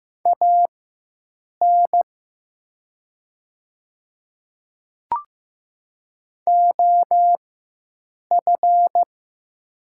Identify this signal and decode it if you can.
Morse code, heard as a keyed tone, about 15 wpm, spaced as for 7 wpm, 700 Hz. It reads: AN OF
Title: 4x Character Spacing - Top 300 Words - 15wpm